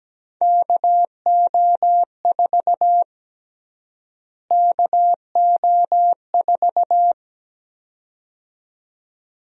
Morse code, heard as a keyed tone, about 17 wpm, 700 Hz. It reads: KO4 KO4
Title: USA Call Sign Prefixes - Repeated 2x - 17wpm